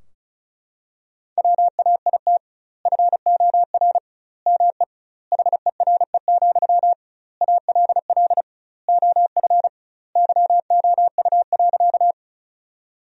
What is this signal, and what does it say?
Morse code, heard as a keyed tone, about 36 wpm, 700 Hz. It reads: WAIT FOR ME HERE, ALL OF YOU.
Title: 2x Word-Spacing - Sentences from Top 500 Words - 35wpm